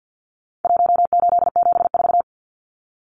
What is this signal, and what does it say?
Morse code, heard as a keyed tone, about 50 wpm, 700 Hz. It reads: 1874
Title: Numbers - Contest Style - 4-Digit 50wpm